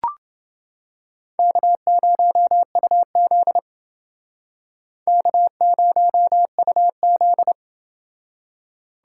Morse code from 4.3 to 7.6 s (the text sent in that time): K0UZ